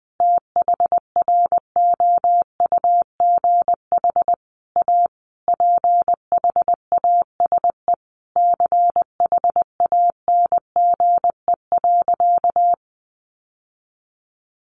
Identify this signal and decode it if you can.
Morse code, heard as a keyed tone, about 19 wpm, 700 Hz. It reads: THROUGH A PHASE CHANGE.
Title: Sentences from Top 1200 Words 20wpm